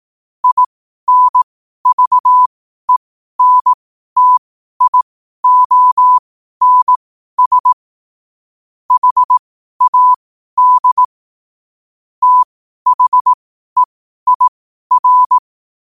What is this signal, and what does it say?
Morse code, heard as a keyed tone, about 18 wpm, spaced as for 12 wpm, 1000 Hz. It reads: INVENTIONS HAD THEIR
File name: qotdfiles / 12wpm